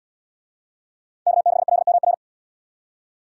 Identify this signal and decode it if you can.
Morse code, heard as a keyed tone, about 77 wpm, 700 Hz. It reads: K6LMR